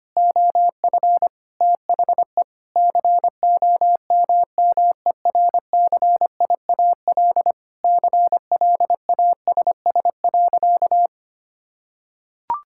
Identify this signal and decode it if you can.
Morse code, heard as a keyed tone, about 25 wpm, 700 Hz. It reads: OF THE COMMERCIAL CLASS.
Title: Sentences from Top 1000 Words - Review Part 5 - 25wpm